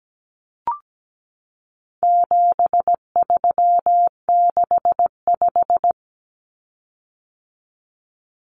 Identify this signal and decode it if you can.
Morse code, heard as a keyed tone, about 17 wpm, 700 Hz. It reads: 7365
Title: Common Cut Numbers - 4-Digit 17wpm